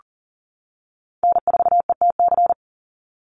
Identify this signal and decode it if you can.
Morse code, heard as a keyed tone, about 41 wpm, 700 Hz. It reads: N4ETC
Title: Call Signs - 40wpm